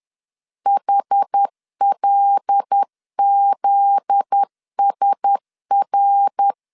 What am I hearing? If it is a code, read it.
Morse code, heard as a keyed tone, about 10 wpm, 800 Hz. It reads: HLZSR
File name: LEZIONI / 7 GF9Z / 7 GF9Z 10wpm 2611